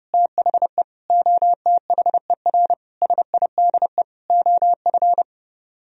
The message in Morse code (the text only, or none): THE OTHER SIDE OF